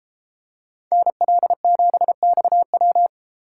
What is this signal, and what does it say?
Morse code, heard as a keyed tone, about 32 wpm, 700 Hz. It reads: NL7XW